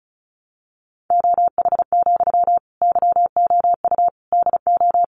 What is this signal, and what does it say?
Morse code, heard as a keyed tone, about 34 wpm, 700 Hz. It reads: OH, YOU DO
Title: Sentences from Top 600 Words 35wpm